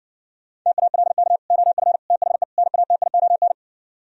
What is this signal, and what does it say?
Morse code, heard as a keyed tone, about 59 wpm, 700 Hz. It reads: TALK OF THE NATION